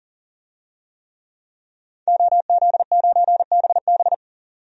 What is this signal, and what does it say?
Morse code, heard as a keyed tone, about 37 wpm, 700 Hz. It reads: OZ9BB
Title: DXCC-100 Call Signs - 5 Characters - 40wpm